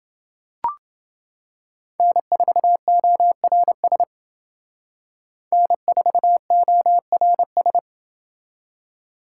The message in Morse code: N4ORS N4ORS

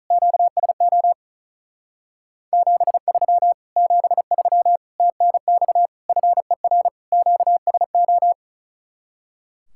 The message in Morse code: QSO 73 73 TNX FER QSO

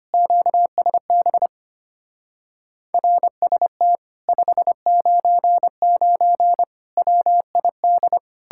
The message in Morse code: QSB RST 599 WID

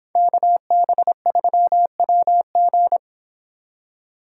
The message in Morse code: KB3WG

